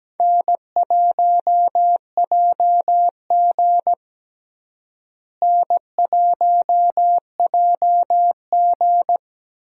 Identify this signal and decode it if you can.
Morse code, heard as a keyed tone, about 17 wpm, 700 Hz. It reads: N1JG N1JG
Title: USA Call Signs - 4 Characters - Repeated 3x - 17wpm